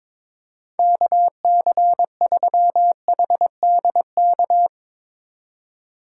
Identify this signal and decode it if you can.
Morse code, heard as a keyed tone, about 22 wpm, 700 Hz. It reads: KC3HDK
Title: USA Call Signs - 6 Characters - 2x Character Spacing - 22wpm